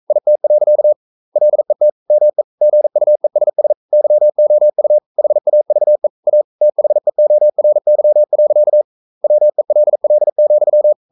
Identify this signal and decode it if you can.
Morse code, heard as a keyed tone, about 42 wpm, 600 Hz. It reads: IT. LET ME GUESS YOU HAVE A THEORY. WELL,
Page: Romanen Never af Ken Follet på engelsk.